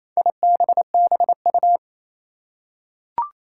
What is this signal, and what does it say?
Morse code, heard as a keyed tone, about 28 wpm, 700 Hz. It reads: IBBU
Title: Four Challenging Letter-Number Combinations 28wpm